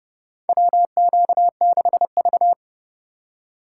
Morse code, heard as a keyed tone, about 30 wpm, 700 Hz. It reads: WQ6V
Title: USA Call Signs - 4 Characters - 8x Character Spacing - 30wpm